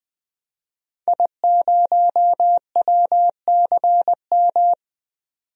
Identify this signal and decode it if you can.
Morse code, heard as a keyed tone, about 21 wpm, 700 Hz. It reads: I0WCM
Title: DXCC-100 Call Signs - 5 Characters - 20wpm